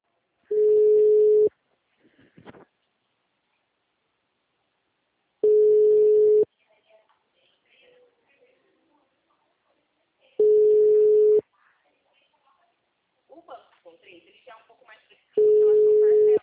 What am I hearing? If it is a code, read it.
Ringback tone (Europe, 425 Hz), with speech.